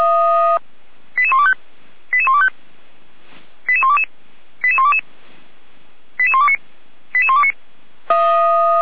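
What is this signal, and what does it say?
Five-tone selective call (ZVEI), heard as quick runs of five tones, tones of about 80 ms. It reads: 88125 88125 88122 88122 88129 88129
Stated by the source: ZVEI five-tone call 88235 received on an SDR, from the Feuerwehr Orlamünde fire brigade server.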